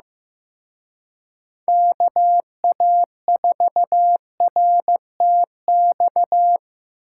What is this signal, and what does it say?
Morse code, heard as a keyed tone, about 18 wpm, 700 Hz. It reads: KA4RTX
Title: Speed Racing - USA 2x3 Call Signs 15wpm